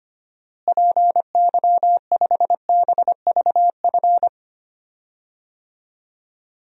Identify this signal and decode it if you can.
Morse code, heard as a keyed tone, about 25 wpm, 700 Hz. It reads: PY5BVF
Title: DXCC-100 Call Signs - 6 Characters - Repeated 3x - 25wpm